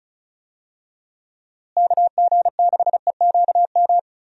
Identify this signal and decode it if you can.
Morse code, heard as a keyed tone, about 36 wpm, 700 Hz. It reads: KG6EQM
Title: USA Call Signs - 6 Characters - Repeated 3x - 35wpm